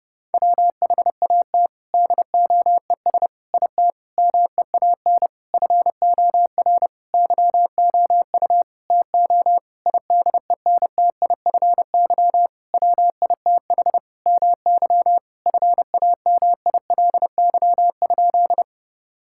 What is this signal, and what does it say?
Morse code, heard as a keyed tone, about 30 wpm, 700 Hz. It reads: WHAT DOES IT MEAN FOR YOU TO IDENTIFY WITH MY FAMILY?
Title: Sentences from Top 1000 Words - Review Part 6 - 30wpm